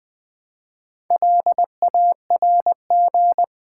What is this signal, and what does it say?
Morse code, heard as a keyed tone, about 20 wpm, 700 Hz. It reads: LARG
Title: Build A Word - 1 Letter At A Time - 5-Letter Words - 20wpm